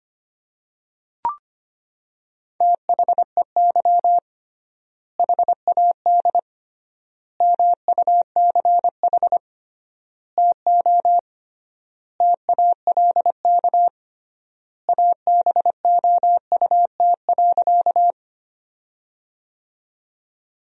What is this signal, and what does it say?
Morse code, heard as a keyed tone, about 25 wpm, 700 Hz. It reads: THEY HAD MUCH TO TALK ABOUT.
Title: 3x Word-Spacing - Sentences from Top 400 Words - 25wpm